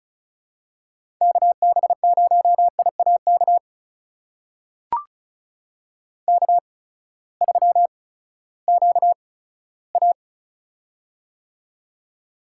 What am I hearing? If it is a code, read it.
Morse code, heard as a keyed tone, about 35 wpm, spaced as for 17 wpm, 700 Hz. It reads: KB0IAK K3QA
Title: USA Call Signs - 8x Character Spacing - 35wpm